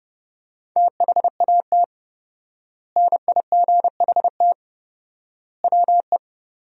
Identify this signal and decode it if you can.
Morse code, heard as a keyed tone, about 30 wpm, 700 Hz. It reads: THAT NIGHT WE
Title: 4x Word-Spacing - Sentences from Top 300 Words - 30wpm